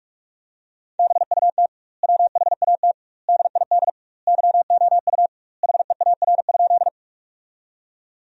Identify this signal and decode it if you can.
Morse code, heard as a keyed tone, about 46 wpm, 700 Hz. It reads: BUT WHAT DID YOU HEAR?
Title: 2x Word-Spacing - Sentences from Top 300 Words - 45wpm